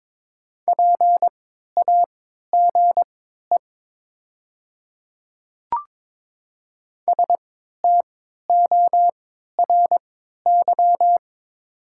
Morse code, heard as a keyed tone, about 22 wpm, spaced as for 12 wpm, 700 Hz. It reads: PAGE STORY
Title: 3x Character Spacing - Top 300 Words - 22wpm